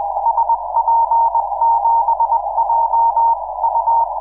Morse code, heard as a keyed tone, about 20 wpm, 920 Hz, with noise. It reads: SP7JU
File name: iw0ffk / 2006 06 11 jj1lgo iw0ffk